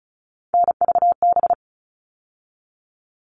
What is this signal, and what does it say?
Morse code, heard as a keyed tone, about 34 wpm, 700 Hz. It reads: NVB
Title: All Cut Numbers - 3-Digit 35wpm